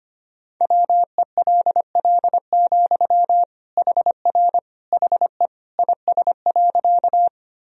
Morse code, heard as a keyed tone, about 25 wpm, 700 Hz. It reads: WELL, HR HE IS.
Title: Sentences from Top 200 Words and Abbreviations 25wpm